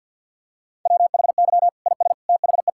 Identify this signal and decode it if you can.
Morse code, heard as a keyed tone, about 48 wpm, 700 Hz. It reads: WHY IS THE
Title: Sentences from Top 1000 Words - Review Part 5 - 50wpm